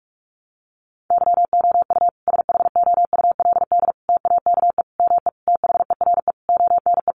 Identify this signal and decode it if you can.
Morse code, heard as a keyed tone, about 44 wpm, 700 Hz. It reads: YOU SHOULD TAKE ME THERE ONE
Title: Sentences from Top 300 Words 45wpm